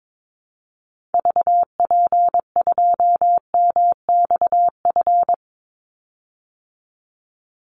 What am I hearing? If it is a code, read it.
Morse code, heard as a keyed tone, about 22 wpm, 700 Hz. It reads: VP2MXF
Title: Call Signs Repeated 3x - 22wpm